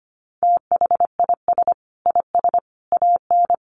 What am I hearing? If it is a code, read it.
Morse code, heard as a keyed tone, about 25 wpm, 700 Hz. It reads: THIS IS AN
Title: Sentences from Top 1300 Words 25wpm